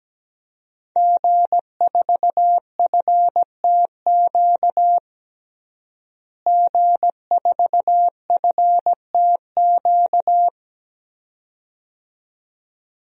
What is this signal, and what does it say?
Morse code, heard as a keyed tone, about 17 wpm, 700 Hz. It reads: G4FTQ G4FTQ
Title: DXCC-100 Call Signs - 5 Characters - Repeated 2x - 17wpm